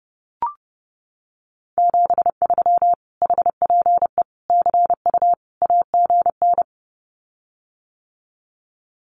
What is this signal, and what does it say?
Morse code, heard as a keyed tone, about 30 wpm, 700 Hz. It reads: 73 HPE CU AGN
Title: Common QSO Phrases Encore 30wpm